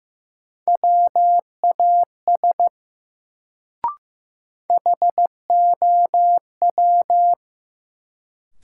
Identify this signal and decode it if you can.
Morse code, heard as a keyed tone, about 15 wpm, 700 Hz. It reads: WAS HOW
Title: Sentences from Top 100 Words - New Word Drill - 15wpm